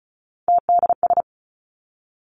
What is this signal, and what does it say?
Morse code, heard as a keyed tone, about 36 wpm, 700 Hz. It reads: TDS
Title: Three Challenging Letter-Number Combinations 35wpm